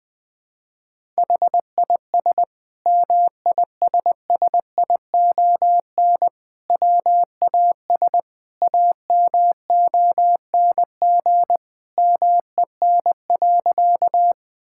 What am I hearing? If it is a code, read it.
Morse code, heard as a keyed tone, about 20 wpm, 700 Hz. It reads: HIS MISSION WAS AMONG MEN.
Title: Sentences from Top 1600 Words 20wpm